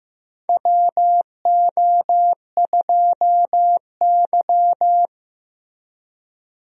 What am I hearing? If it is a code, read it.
Morse code, heard as a keyed tone, about 15 wpm, 700 Hz. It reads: WO2Y
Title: USA Call Signs - 4 Characters - Repeated 3x - 15wpm